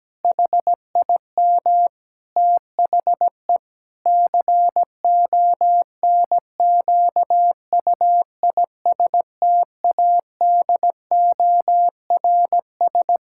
Morse code, heard as a keyed tone, about 17 wpm, 700 Hz. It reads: HIM THE CONQUISTADORS
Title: Pangram Challenge v2 - 17wpm